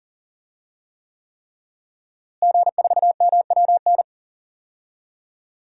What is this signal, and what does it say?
Morse code, heard as a keyed tone, about 41 wpm, 700 Hz. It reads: G4MWN
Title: DXCC-100 Call Signs - 5 Characters - Repeated 3x - 40wpm